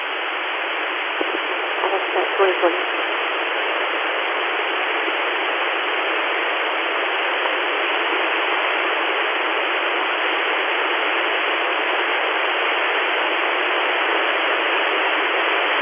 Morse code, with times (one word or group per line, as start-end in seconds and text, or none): none